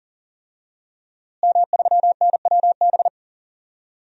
Morse code, heard as a keyed tone, about 40 wpm, 700 Hz. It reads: M3NWB
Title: DXCC-100 Call Signs - 5 Characters - Repeated 3x - 40wpm